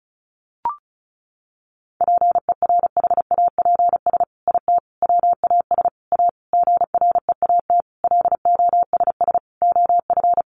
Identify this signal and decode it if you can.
Morse code, heard as a keyed tone, about 34 wpm, 700 Hz. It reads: PERHAPS IT WAS A GREAT LOSS OF
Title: Sentences from Top 1400 Words 35wpm